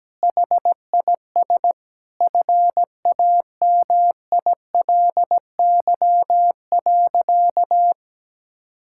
Morse code, heard as a keyed tone, about 17 wpm, 700 Hz. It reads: HIS FAMILY.
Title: Sentences from Top 1000 Words - Review Part 5 - 17wpm